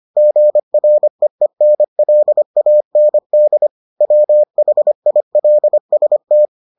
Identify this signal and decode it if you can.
Morse code, heard as a keyed tone, about 25 wpm, 600 Hz. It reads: GREENLAND WHILST